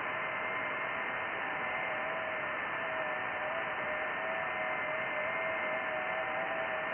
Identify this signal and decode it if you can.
FT8.
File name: IC-7100